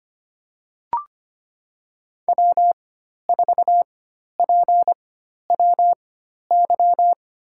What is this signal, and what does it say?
Morse code, heard as a keyed tone, about 25 wpm, spaced as for 12 wpm, 700 Hz. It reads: W4PWY
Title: USA Call Signs - 4x Character Spacing - 25wpm